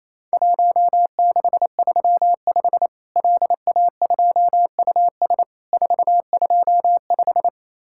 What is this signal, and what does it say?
Morse code, heard as a keyed tone, about 27 wpm, 700 Hz. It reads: 1635 LA2US 425
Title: Worked All Europe DX Contest - QTC Exchange - 28wpm